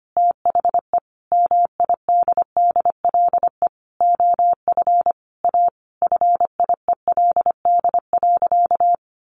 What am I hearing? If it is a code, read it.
Morse code, heard as a keyed tone, about 25 wpm, 700 Hz. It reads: THE MIDDLE OF A FIELD.
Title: Sentences from Top 1000 Words 25wpm